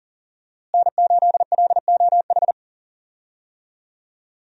Morse code, heard as a keyed tone, about 38 wpm, 700 Hz. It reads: N8LOH
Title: USA Call Signs - 5 Characters - Repeated 2x - 40wpm